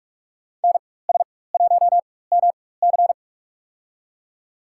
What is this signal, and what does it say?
Morse code, heard as a keyed tone, about 47 wpm, spaced as for 21 wpm, 700 Hz. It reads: NS1MC
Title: USA Call Signs - 5 Characters - 4x Character Spacing - 45wpm